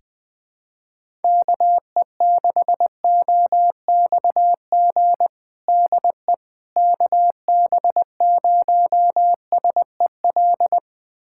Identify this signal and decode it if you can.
Morse code, heard as a keyed tone, about 20 wpm, 700 Hz. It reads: KE6OXG DE KB0SEL
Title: USA Callsign DE Callsign - 6 Characters - 20wpm